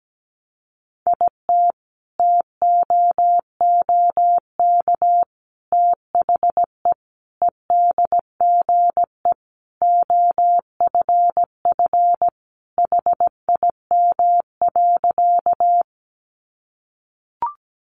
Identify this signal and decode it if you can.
Morse code, heard as a keyed tone, about 17 wpm, 700 Hz. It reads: IT TOOK THE EDGE OFF HIM.
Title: Sentences from Top 1300 Words 17wpm